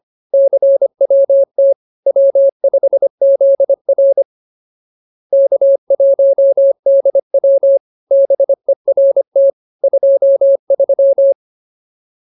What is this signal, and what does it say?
Morse code, heard as a keyed tone, about 25 wpm, 550 Hz. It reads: CWT W5ZR K1DW BERT 23
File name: CWT-209-25